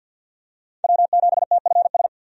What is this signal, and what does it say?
Morse code, heard as a keyed tone, about 52 wpm, 700 Hz. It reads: W7TFS
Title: USA Call Signs - 5 Characters - 50wpm